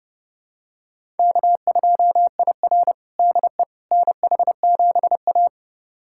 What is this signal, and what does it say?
Morse code, heard as a keyed tone, about 30 wpm, 700 Hz. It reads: K2IR DE NH7A
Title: USA Callsign DE Callsign - 4 Characters - 30wpm